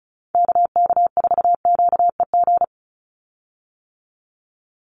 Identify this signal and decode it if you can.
Morse code, heard as a keyed tone, about 34 wpm, 700 Hz. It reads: KK4QEG
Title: USA Call Signs 35wpm